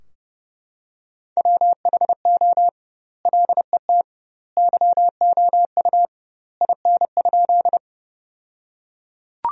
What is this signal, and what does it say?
Morse code, heard as a keyed tone, about 30 wpm, 700 Hz. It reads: WHO LET YOU IN?
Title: 2x Word-Spacing - Sentences from Top 300 Words - 30wpm